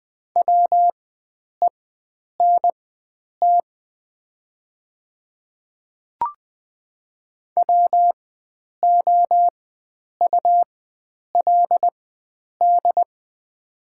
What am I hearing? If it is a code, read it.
Morse code, heard as a keyed tone, about 20 wpm, spaced as for 9 wpm, 700 Hz. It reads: WENT WOULD
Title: 4x Character Spacing - Top 500 Words - 20wpm